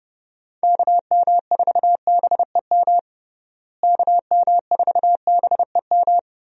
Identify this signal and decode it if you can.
Morse code, heard as a keyed tone, about 30 wpm, 700 Hz. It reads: KM4BEM KM4BEM
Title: USA Call Signs - 6 Characters - Repeated 2x - 30wpm